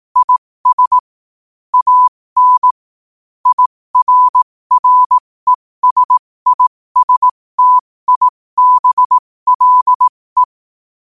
Morse code, heard as a keyed tone, about 18 wpm, spaced as for 15 wpm, 1000 Hz. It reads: IS AN IRRESISTIBLE
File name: qotdfiles / 15wpm